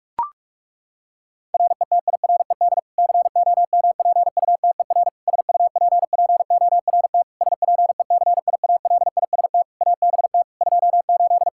Morse code, heard as a keyed tone, about 41 wpm, 700 Hz. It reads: RETIRED COMPUTER SUPPORT SPECIALIST ABT 28
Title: Practice QSO - Exchange 3 - More Info - 45wpm